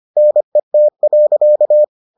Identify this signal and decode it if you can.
Morse code, heard as a keyed tone, about 25 wpm, 600 Hz. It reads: NET.